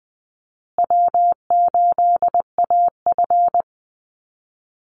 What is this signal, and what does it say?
Morse code, heard as a keyed tone, about 20 wpm, 700 Hz. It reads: W8AF
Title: USA Call Signs Repeated 2x 20wpm